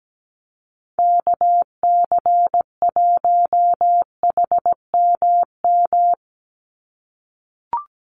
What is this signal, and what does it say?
Morse code, heard as a keyed tone, about 17 wpm, 700 Hz. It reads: KC1HMM
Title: USA Call Signs 17wpm